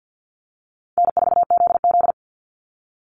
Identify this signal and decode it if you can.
Morse code, heard as a keyed tone, about 48 wpm, 700 Hz. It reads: N4ZZ
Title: Easy Call Signs Repeated 2x - 50wpm